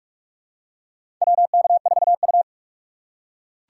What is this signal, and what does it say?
Morse code, heard as a keyed tone, about 41 wpm, 700 Hz. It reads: WK4U
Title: USA Call Signs - 4 Characters - 4x Character Spacing - 45wpm